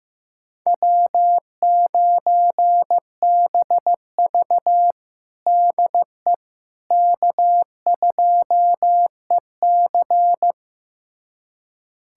Morse code, heard as a keyed tone, about 16 wpm, 700 Hz. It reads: W9BV DE K2EC
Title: USA Callsign DE Callsign - 4 Characters - 15wpm